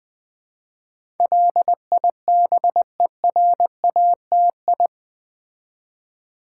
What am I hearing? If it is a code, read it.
Morse code, heard as a keyed tone, about 20 wpm, 700 Hz. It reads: LIBERATI
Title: Build A Word - 4 Letters At A Time - 10-Letter Words - 20wpm